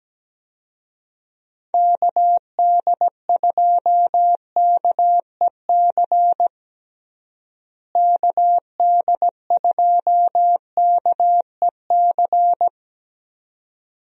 Morse code, heard as a keyed tone, about 17 wpm, 700 Hz. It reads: KD2KEC KD2KEC
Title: USA Call Signs - 6 Characters - Repeated 2x - 17wpm